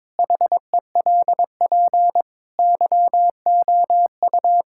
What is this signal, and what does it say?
Morse code, heard as a keyed tone, about 22 wpm, 700 Hz. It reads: HELP YOU
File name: Difficult Sentences with Hints - 22wpm